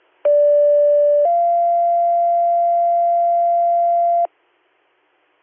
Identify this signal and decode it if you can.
Two-tone page, heard as a one-second tone, then a three-second tone, about 580 then 690 Hz. It is heard from 0.3 to 4.3 s.